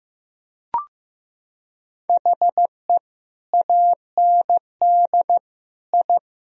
Morse code, heard as a keyed tone, about 15 wpm, 700 Hz. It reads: HE AND I
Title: Sentences from Top 1000 Words - Review Part 3 - 15wpm